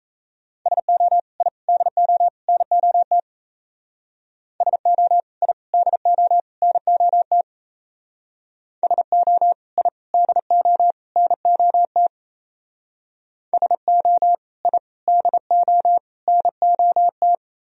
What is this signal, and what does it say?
Morse code, heard as a keyed tone, about 33 wpm, 700 Hz. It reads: SO I DO NOT SO I DO NOT SO I DO NOT SO I DO NOT
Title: Speed-Racing - Sets of 4 Words 28wpm